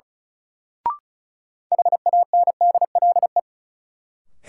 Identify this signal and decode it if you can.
Morse code, heard as a keyed tone, about 34 wpm, 700 Hz. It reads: HANDLE
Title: Sentences from Top 1100 Words - New Word Drill Alt - 35wpm